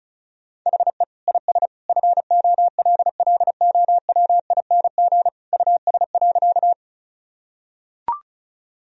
Morse code, heard as a keyed tone, about 36 wpm, 700 Hz. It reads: HE IS FOLLOWING US.
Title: Sentences from Top 1000 Words - Review Part 4 - 35wpm